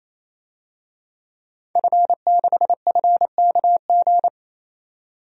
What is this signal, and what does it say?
Morse code, heard as a keyed tone, about 28 wpm, 700 Hz. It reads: F6FKG
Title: DXCC-100 Call Signs - 5 Characters - 28wpm